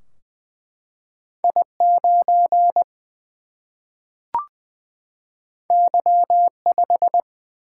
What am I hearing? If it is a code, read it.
Morse code, heard as a keyed tone, about 20 wpm, 700 Hz. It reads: I 9 Y 5
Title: Single-Letter Single-Number 20wpm